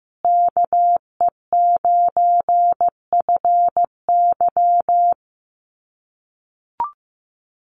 Morse code, heard as a keyed tone, about 15 wpm, 700 Hz. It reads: KE9FY
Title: USA Call Signs Repeated 2x 15wpm